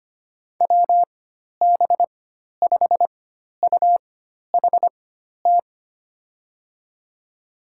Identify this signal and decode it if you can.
Morse code, heard as a keyed tone, about 25 wpm, spaced as for 12 wpm, 700 Hz. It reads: WB5UHT
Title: USA Call Signs - 6 Characters - 4x Character Spacing - 25wpm